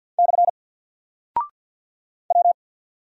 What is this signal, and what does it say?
Morse code, heard as a keyed tone, about 48 wpm, 700 Hz. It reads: / W
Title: Single Character (Letter:Number:Punctuation:Prosign) Alt 50wpm